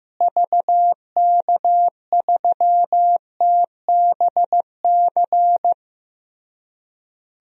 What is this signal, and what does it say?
Morse code, heard as a keyed tone, about 15 wpm, 700 Hz. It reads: VK3TBC
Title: DXCC-100 Call Signs - 6 Characters - Repeated 3x - 15wpm